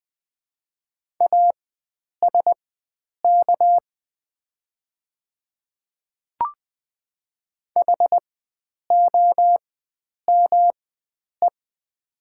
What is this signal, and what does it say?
Morse code, heard as a keyed tone, about 20 wpm, spaced as for 9 wpm, 700 Hz. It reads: ASK HOME
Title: 4x Character Spacing - Top 300 Words - 20wpm